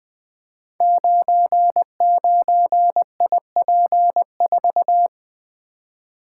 Morse code, heard as a keyed tone, about 20 wpm, 700 Hz. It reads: 99IP4
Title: Code Groups - 5 Letter:Numbers - 20wpm